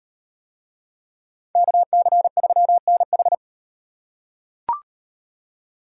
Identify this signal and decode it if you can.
Morse code, heard as a keyed tone, about 38 wpm, 700 Hz. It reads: KC3NH